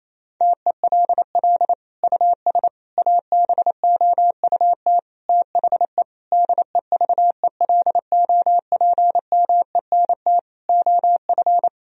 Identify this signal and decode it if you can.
Morse code, heard as a keyed tone, about 28 wpm, 700 Hz. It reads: TELL US ABOUT THE DEVELOPMENT OF
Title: Sentences from Top 1000 Words - Review Part 4 - 28wpm